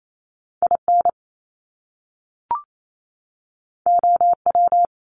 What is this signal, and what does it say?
Morse code, heard as a keyed tone, about 28 wpm, 700 Hz. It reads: IN OW
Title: Top Two Letter Combinations in Words 28wpm